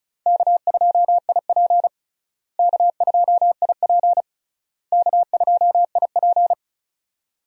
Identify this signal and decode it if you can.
Morse code, heard as a keyed tone, about 36 wpm, 700 Hz. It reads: K2IP K2IP K2IP
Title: USA Call Signs - 4 Characters - Repeated 3x - 35wpm